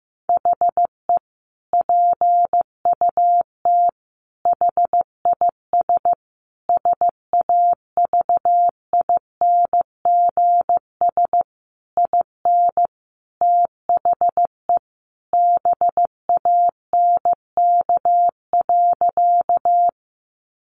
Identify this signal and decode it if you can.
Morse code, heard as a keyed tone, about 15 wpm, 700 Hz. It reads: HE PUT HIS SAVINGS IN THE BANK.
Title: Sentences from Top 1200 Words 15wpm